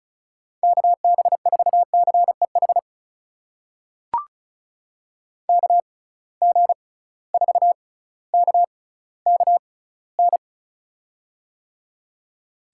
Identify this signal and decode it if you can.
Morse code, heard as a keyed tone, about 34 wpm, spaced as for 18 wpm, 700 Hz. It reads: KB4CEH KG4KKN
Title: USA Call Signs - 6 Characters - 6x Character Spacing - 35wpm